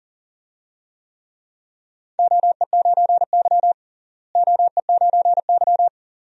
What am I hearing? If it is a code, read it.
Morse code, heard as a keyed tone, about 39 wpm, 700 Hz. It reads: OE9Y OE9Y
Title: DXCC-100 Call Signs - 4 Characters - Repeated 2x - 40wpm